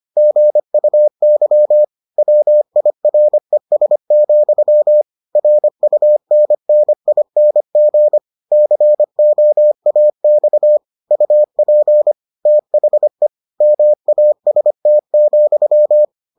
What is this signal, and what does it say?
Morse code, heard as a keyed tone, about 25 wpm, 600 Hz. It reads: GUY WIRES, RUNNING COAX UP THE MAST,